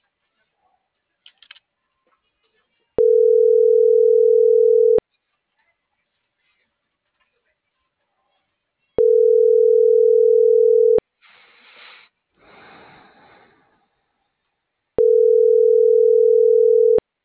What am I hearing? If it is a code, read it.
Ringback tone (North America), 440 + 480 Hz.